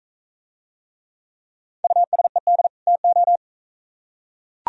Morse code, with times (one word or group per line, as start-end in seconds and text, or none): 1.8-2.7 USED
2.9-3.4 TO